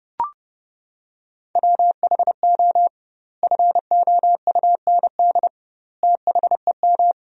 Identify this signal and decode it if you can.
Morse code, heard as a keyed tone, about 30 wpm, 700 Hz. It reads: WHO FOUND THEM
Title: 2x Word-Spacing - Sentences from Top 300 Words - 30wpm